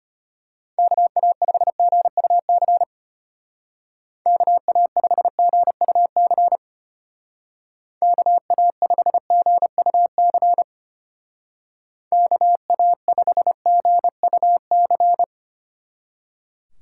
Morse code, heard as a keyed tone, about 30 wpm, 700 Hz. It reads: KA5GUC KA5GUC KA5GUC KA5GUC
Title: Speed Racing - USA 2x3 Call Signs 25wpm